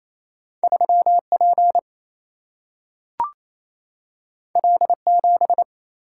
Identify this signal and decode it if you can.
Morse code, heard as a keyed tone, about 28 wpm, 700 Hz. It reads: 3 P L 7
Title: Single-Letter Single-Number 28wpm